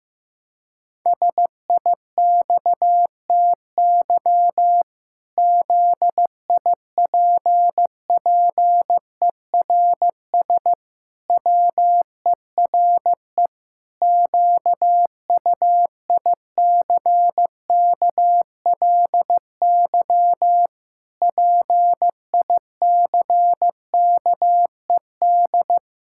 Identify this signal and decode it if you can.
Morse code, heard as a keyed tone, about 15 wpm, 700 Hz. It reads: SIXTY ZIPPERS WERE QUICKLY PICKED
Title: Pangram Challenge v2 - 15wpm